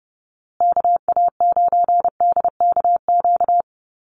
Morse code, heard as a keyed tone, about 30 wpm, 700 Hz. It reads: KA9DKQ
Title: USA Call Signs Repeated 3x 30wpm